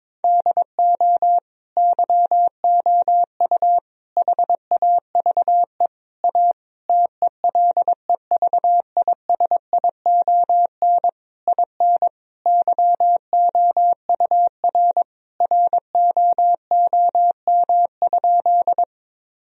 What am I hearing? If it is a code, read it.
Morse code, heard as a keyed tone, about 22 wpm, 700 Hz. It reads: DO YOU HAVE A TELEVISION IN YOUR ROOM?
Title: Sentences from Top 1000 Words - Review Part 4 - 22wpm